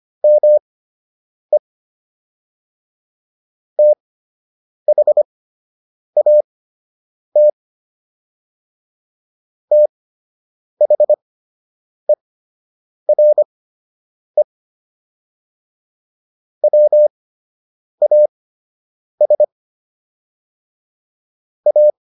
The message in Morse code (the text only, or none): ME THAT THERE WAS A